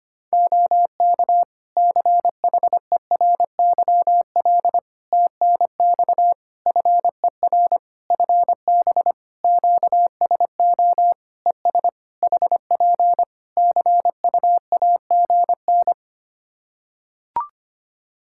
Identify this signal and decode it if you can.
Morse code, heard as a keyed tone, about 25 wpm, 700 Hz. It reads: OK CHERYL TNX FER FB QSO ES HP CUAGN
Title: Practice QSO - Exchange 4 - Ending - 25wpm